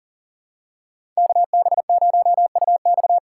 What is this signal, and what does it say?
Morse code, heard as a keyed tone, about 40 wpm, 700 Hz. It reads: KB0UX
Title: USA Call Signs - 5 Characters - 2x Character Spacing - 40wpm